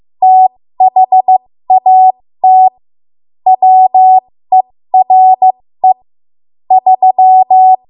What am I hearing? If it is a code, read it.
Morse code, heard as a keyed tone, about 15 wpm, spaced as for 13 wpm, 750 Hz. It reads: THAT WERE 3
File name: Morse / 13WPM